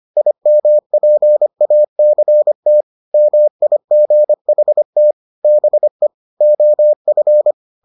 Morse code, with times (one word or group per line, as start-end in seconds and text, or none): 0.2-2.8 IMPACT
3.1-5.1 MIGHT
5.4-6.1 BE
6.4-7.5 OF